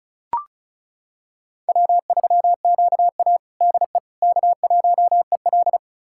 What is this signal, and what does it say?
Morse code, heard as a keyed tone, about 36 wpm, 690 Hz. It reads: W3QA DE K1EL
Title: USA Callsign DE Callsign - 4 Characters - 35wpm